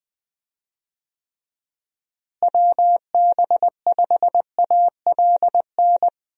Morse code, heard as a keyed tone, about 21 wpm, 700 Hz. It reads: WB5ALN
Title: USA Call Signs - 6 Characters - Repeated 2x - 20wpm